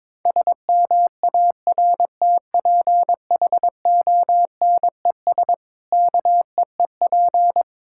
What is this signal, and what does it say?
Morse code, heard as a keyed tone, about 22 wpm, 700 Hz. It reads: SMARTPHONES KEEP
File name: Difficult Sentences with Hints - 22wpm